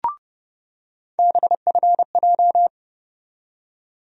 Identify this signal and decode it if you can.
Morse code, heard as a keyed tone, about 32 wpm, 700 Hz. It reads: BFJ
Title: USA Call Sign Postfixes - 3 Letters 30wpm